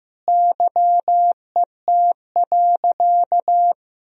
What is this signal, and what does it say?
Morse code, heard as a keyed tone, about 15 wpm, 700 Hz. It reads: YET.
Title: Sentences from Top 1000 Words - Review Part 2 - 15wpm